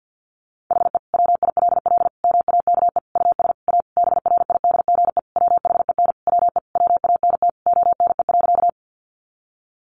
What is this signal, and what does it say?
Morse code, heard as a keyed tone, about 50 wpm, 700 Hz. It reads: HE WILL MAKE US A BRIDGE WHEN WE WANT ONE.
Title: Sentences from Top 1200 Words 50wpm